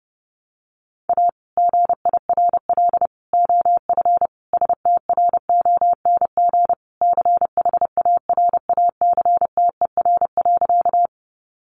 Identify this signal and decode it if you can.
Morse code, heard as a keyed tone, about 30 wpm, 700 Hz. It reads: A GIRL OF STRONG CHARACTER.
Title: Sentences from Top 1200 Words 30wpm